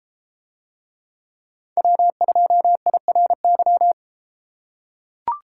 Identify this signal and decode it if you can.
Morse code, heard as a keyed tone, about 33 wpm, 700 Hz. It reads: W2IRY